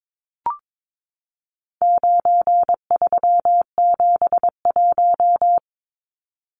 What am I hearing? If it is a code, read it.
Morse code, heard as a keyed tone, about 22 wpm, 700 Hz. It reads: 9371
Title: Numbers - 4-Digit 22wpm